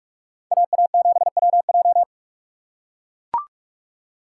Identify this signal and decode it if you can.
Morse code, heard as a keyed tone, about 43 wpm, 700 Hz. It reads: AA7WJ